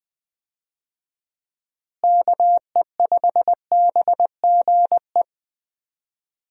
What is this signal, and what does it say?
Morse code, heard as a keyed tone, about 20 wpm, 700 Hz. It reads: KE5BGE
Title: USA Call Signs - 6 Characters - Repeated 2x - 20wpm